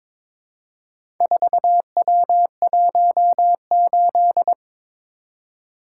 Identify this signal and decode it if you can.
Morse code, heard as a keyed tone, about 22 wpm, 700 Hz. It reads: 4W18
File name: practice / 22wpm / Four Letter-Number Combinations - 22wpm